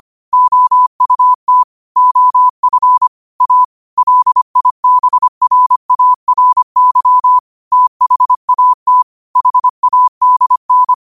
Morse code, heard as a keyed tone, about 25 wpm, 1000 Hz. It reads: OUT OF A LIBRARY THAT HADN
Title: 25 WPM morse code quotes for Thu, 29 Feb 2024